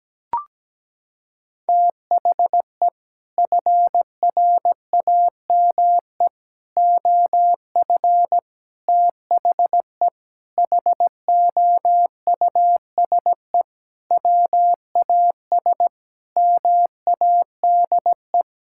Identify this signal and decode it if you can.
Morse code, heard as a keyed tone, about 17 wpm, 700 Hz. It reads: THE FRAME OF THE HOUSE WAS MADE
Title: Sentences from Top 1000 Words - Review Part 6 - 17wpm